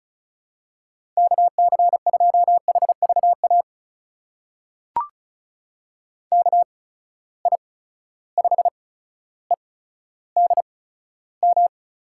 Morse code, heard as a keyed tone, about 36 wpm, spaced as for 15 wpm, 700 Hz. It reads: KC2HVA KI5EDM
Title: USA Call Signs - 6 Characters - 8x Character Spacing - 35wpm